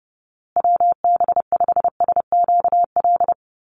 Morse code, heard as a keyed tone, about 30 wpm, 700 Hz. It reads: WB5SQL
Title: USA Call Signs Repeated 3x 30wpm